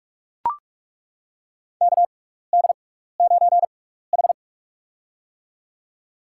Morse code, heard as a keyed tone, about 42 wpm, spaced as for 15 wpm, 700 Hz. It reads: KD9H
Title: USA Call Signs - 4 Characters - 6x Character Spacing - 45wpm